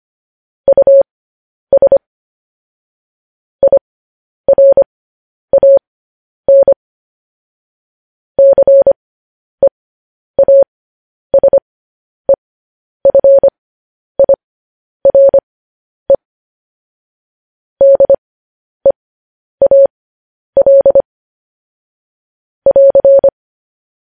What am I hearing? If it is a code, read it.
Morse code, heard as a keyed tone, about 25 wpm, spaced as for 10 wpm, 550 Hz. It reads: US IRAN CEASEFIRE DEAL <AR>